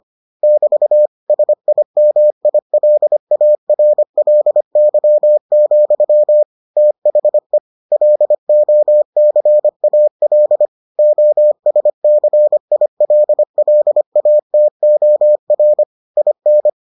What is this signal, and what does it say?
Morse code, heard as a keyed tone, about 25 wpm, 600 Hz. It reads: <BT> SIMILARLY, THE LOCAL OSCILLATOR IN